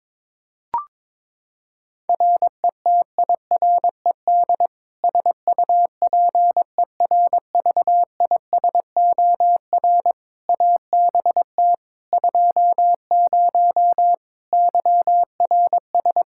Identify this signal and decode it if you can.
Morse code, heard as a keyed tone, about 22 wpm, 700 Hz. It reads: RETIRED SUPERVISOR ABT 20 YRS
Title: Practice QSO - Exchange 3 - More Info - 22wpm